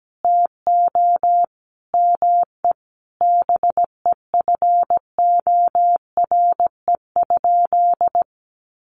Morse code, heard as a keyed tone, about 17 wpm, 700 Hz. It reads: TO ME BEFORE?
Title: Sentences from Top 900 Words 17wpm